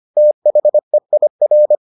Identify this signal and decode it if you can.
Morse code, heard as a keyed tone, about 25 wpm, 600 Hz. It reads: THEIR